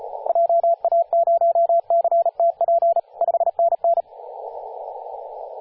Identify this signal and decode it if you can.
Morse code, heard as a keyed tone, about 35 wpm, 680 Hz, with radio noise. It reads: JA0CTP 5NN
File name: posts / 2016-sep-22-0537z-kh0-kw2x-18-077mhz-cw